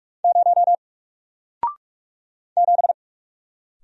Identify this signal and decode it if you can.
Morse code, heard as a keyed tone, about 46 wpm, 700 Hz. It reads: 0 7
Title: Single Letter-Number Alt 45wpm